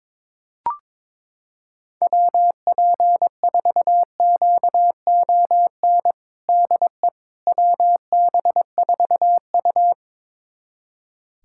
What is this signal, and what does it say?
Morse code, heard as a keyed tone, about 22 wpm, 700 Hz. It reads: WP4QON DE WB4U
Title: USA Callsign DE Callsign 22wpm